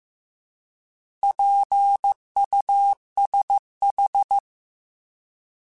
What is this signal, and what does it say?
Morse code, heard as a keyed tone, about 15 wpm, 800 Hz. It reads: PUSH
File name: Alphabetical / Alphabetical mp3s / Ps @13